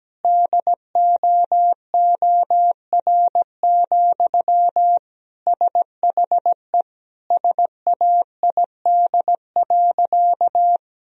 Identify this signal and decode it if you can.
Morse code, heard as a keyed tone, about 17 wpm, 700 Hz. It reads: DOOR, SHE SAID.
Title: Sentences from Top 1000 Words - Review Part 2 - 17wpm